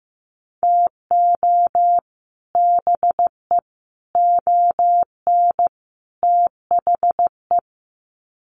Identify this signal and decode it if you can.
Morse code, heard as a keyed tone, about 15 wpm, 700 Hz. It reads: TO BE ON THE
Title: Sets of 4 Words 15wpm